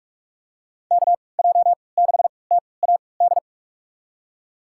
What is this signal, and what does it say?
Morse code, heard as a keyed tone, about 45 wpm, spaced as for 25 wpm, 700 Hz. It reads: KJ6TAD
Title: USA Call Signs - 6 Characters - 3x Character Spacing - 45wpm